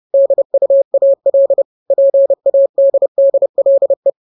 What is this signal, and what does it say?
Morse code, heard as a keyed tone, about 30 wpm, 550 Hz. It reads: DUAL PADDLE
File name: jokes / joke 1 30 0000